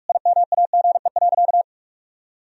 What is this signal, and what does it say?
Morse code, heard as a keyed tone, about 46 wpm, 700 Hz. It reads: IMAGE.
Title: Sentences from Top 1000 Words - Review Part 4 - 45wpm